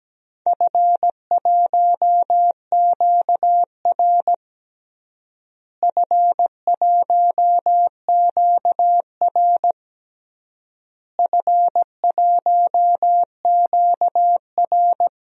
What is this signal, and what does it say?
Morse code, heard as a keyed tone, about 17 wpm, 700 Hz. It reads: F1QR F1QR F1QR
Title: DXCC-100 Call Signs - 4 Characters - Repeated 3x - 17wpm